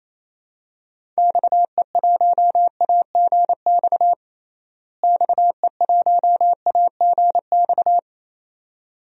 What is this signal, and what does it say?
Morse code, heard as a keyed tone, about 28 wpm, 700 Hz. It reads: XE1AGX XE1AGX
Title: DXCC-100 Call Signs - 6 Characters - Repeated 2x - 28wpm